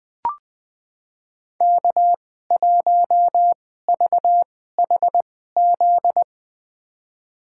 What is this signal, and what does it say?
Morse code, heard as a keyed tone, about 20 wpm, spaced as for 14 wpm, 700 Hz. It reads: K1VHZ
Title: USA Call Signs - 2x Character Spacing - 20wpm